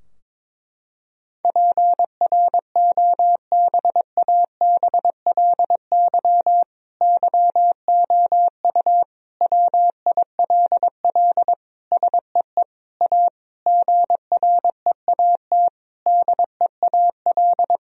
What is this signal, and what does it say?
Morse code, heard as a keyed tone, about 22 wpm, 700 Hz. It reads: PROBABLY YOU WILL SEE A GREAT DEAL
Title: Sentences from Top 1000 Words - Review Part 4 - 22wpm